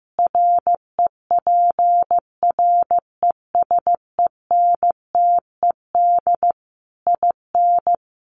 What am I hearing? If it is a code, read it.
Morse code, heard as a keyed tone, about 15 wpm, 700 Hz. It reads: REPRESENTED IN
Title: Sets of 5 Words 15wpm